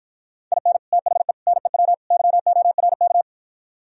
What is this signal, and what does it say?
Morse code, heard as a keyed tone, about 52 wpm, 700 Hz. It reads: IN THE NEW YORK